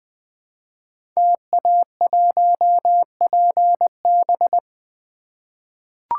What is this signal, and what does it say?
Morse code, heard as a keyed tone, about 20 wpm, 700 Hz. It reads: TA1PB
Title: Code Groups - 5 Letter:Numbers - 20wpm